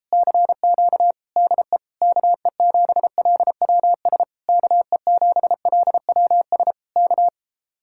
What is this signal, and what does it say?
Morse code, heard as a keyed tone, about 38 wpm, 700 Hz. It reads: CQ DE KE7LWS KE7LWS K